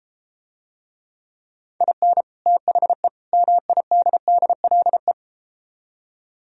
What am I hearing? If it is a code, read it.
Morse code, heard as a keyed tone, about 32 wpm, 700 Hz. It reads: IN THE MIDDLE